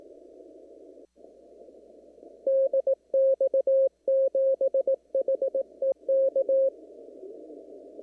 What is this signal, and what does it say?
Morse code, heard as a keyed tone, about 18 wpm, 540 Hz. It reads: DX7HEK
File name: CW / kt5j wpx cw-op-k5pi